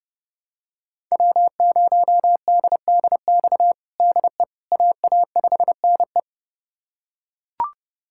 Morse code, heard as a keyed tone, about 30 wpm, 700 Hz. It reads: W0DDX DE AA5NE
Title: USA Callsign DE Callsign - 5 Characters - 30wpm